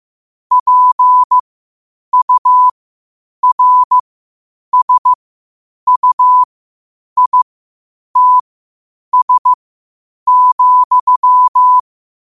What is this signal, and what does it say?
Morse code, heard as a keyed tone, about 15 wpm, spaced as for 8 wpm, 1000 Hz. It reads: PURSUITS,
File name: qotdfiles / 8wpm